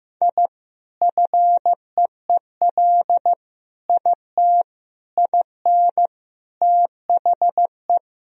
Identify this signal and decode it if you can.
Morse code, heard as a keyed tone, about 15 wpm, 700 Hz. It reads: I FEEL IT IN THE
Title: Sentences from Top 1000 Words - Review Part 2 - 15wpm